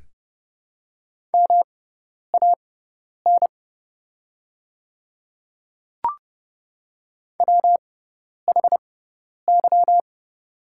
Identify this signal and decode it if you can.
Morse code, heard as a keyed tone, about 30 wpm, spaced as for 10 wpm, 700 Hz. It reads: MAN WHY
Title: 6x Character Spacing - Top 200 Words - 30wpm